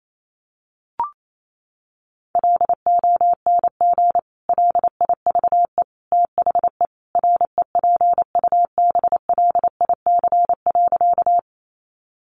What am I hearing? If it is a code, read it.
Morse code, heard as a keyed tone, about 28 wpm, 700 Hz. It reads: LONG LIVE THE REPUBLIC.
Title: Sentences from Top 1500 Words 28wpm